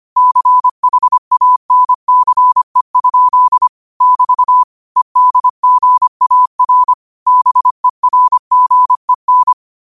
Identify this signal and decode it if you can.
Morse code, heard as a keyed tone, about 25 wpm, 1000 Hz. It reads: CHANCE? <BT> EDGAR BERGEN
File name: qotdfiles / 25wpm